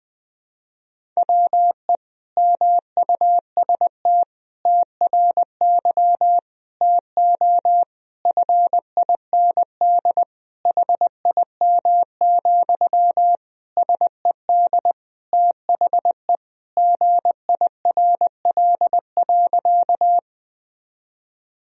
Morse code, heard as a keyed tone, about 20 wpm, 700 Hz. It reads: WE MUST TRY TO FIND HIM, SED THE GIRL.
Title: Sentences from Top 400 Words and Abbreviations 20wpm